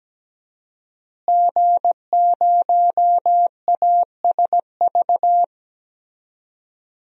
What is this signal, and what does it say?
Morse code, heard as a keyed tone, about 17 wpm, 700 Hz. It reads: G0ASV
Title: DXCC-100 Call Signs - 5 Characters - Repeated 3x - 17wpm